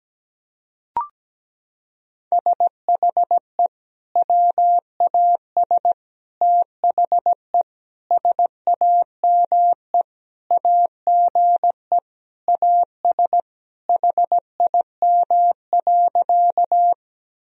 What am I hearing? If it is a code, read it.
Morse code, heard as a keyed tone, about 17 wpm, 700 Hz. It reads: SHE WAS THE SAME AGE AS HIM.
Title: Sentences from Top 1000 Words - Review Part 3 - 17wpm